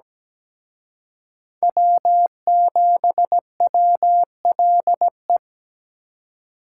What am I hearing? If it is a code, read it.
Morse code, heard as a keyed tone, about 20 wpm, 700 Hz. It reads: W7WLE
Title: Speed Racing - USA 1x3 Call Signs 17wpm